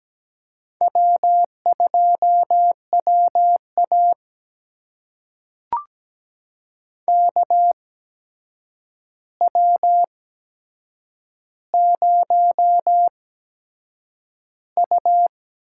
Morse code, heard as a keyed tone, about 17 wpm, spaced as for 7 wpm, 700 Hz. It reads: W2WA KW0U
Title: USA Call Signs - 4 Characters - 8x Character Spacing - 17wpm